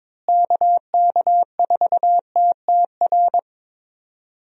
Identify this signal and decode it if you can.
Morse code, heard as a keyed tone, about 22 wpm, 700 Hz. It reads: KK4TTR
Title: USA Call Signs - 6 Characters - Repeated 2x - 22wpm